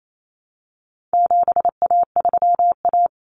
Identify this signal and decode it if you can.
Morse code, heard as a keyed tone, about 28 wpm, 700 Hz. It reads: 7A3A
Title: Common Cut Numbers - 4-Digit 28wpm